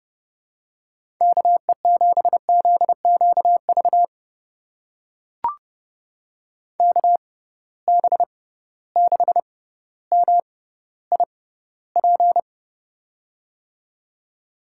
Morse code, heard as a keyed tone, about 30 wpm, spaced as for 15 wpm, 700 Hz. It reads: KE7ZQV KB6MIP
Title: USA Call Signs - 6 Characters - 6x Character Spacing - 30wpm